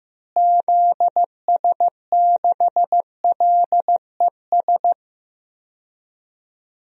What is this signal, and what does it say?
Morse code, heard as a keyed tone, about 15 wpm, 700 Hz. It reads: ZS6LES
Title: DXCC-100 Call Signs - 6 Characters - Repeated 3x - 15wpm